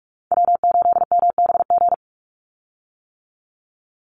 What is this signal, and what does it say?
Morse code, heard as a keyed tone, about 44 wpm, 700 Hz. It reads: W8MBG
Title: USA Call Signs 45wpm